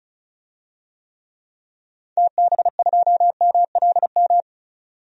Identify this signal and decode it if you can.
Morse code, heard as a keyed tone, about 34 wpm, 700 Hz. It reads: TB2MLM
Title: DXCC-100 Call Signs - 6 Characters - 35wpm